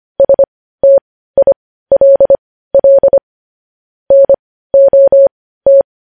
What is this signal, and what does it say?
Morse code, heard as a keyed tone, about 25 wpm, spaced as for 15 wpm, 550 Hz. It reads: STILL NOT
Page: news headlines in morse code at 15 wpm